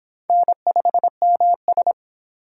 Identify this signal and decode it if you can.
Morse code, heard as a keyed tone, about 26 wpm, 700 Hz. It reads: N5MS